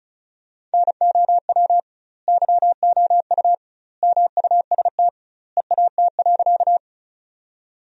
Morse code, heard as a keyed tone, about 36 wpm, 700 Hz. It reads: NOW YOU MUST EAT.
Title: 2x Word-Spacing - Sentences from Top 300 Words - 35wpm